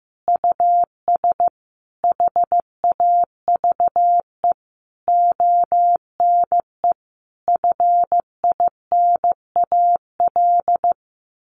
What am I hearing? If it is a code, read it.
Morse code, heard as a keyed tone, about 15 wpm, 700 Hz. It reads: US HAVE ONE FINAL
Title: Sentences from Top 1000 Words 15wpm